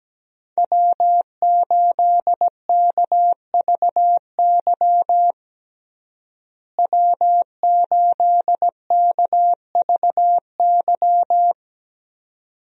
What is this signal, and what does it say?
Morse code, heard as a keyed tone, about 17 wpm, 700 Hz. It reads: W8KVY W8KVY
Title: USA Call Signs - 5 Characters - Repeated 2x - 17wpm